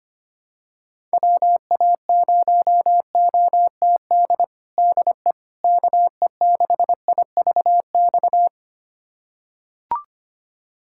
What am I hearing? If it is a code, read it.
Morse code, heard as a keyed tone, about 25 wpm, 700 Hz. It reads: WA0OTD DE KE6IVX
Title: USA Callsign DE Callsign 25wpm